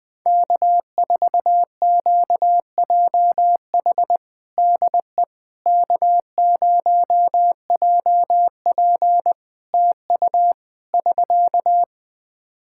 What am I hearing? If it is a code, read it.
Morse code, heard as a keyed tone, about 20 wpm, 700 Hz. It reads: K4QJH DE K0JP TU <SK>
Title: Practice QSO - Exchange 4 - Ending - 20wpm